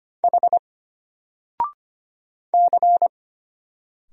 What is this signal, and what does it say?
Morse code, heard as a keyed tone, about 25 wpm, 700 Hz. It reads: H C
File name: Single-Letter-Number-25wpm